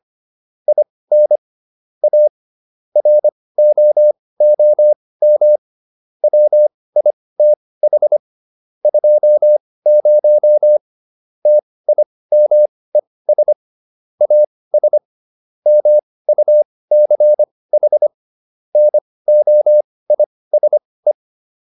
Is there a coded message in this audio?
Yes: Morse code from 0.7 to 21.1 s, heard as a keyed tone, about 25 wpm, spaced as for 18 wpm, 600 Hz. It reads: IN A ROOM WITH 20 TIMES AS MUCH NOISE